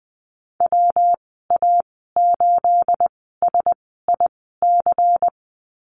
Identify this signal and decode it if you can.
Morse code, heard as a keyed tone, about 20 wpm, spaced as for 14 wpm, 700 Hz. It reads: WA8SIC
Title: 231 Lesson 37g - Letter x - Call Signs 20wpm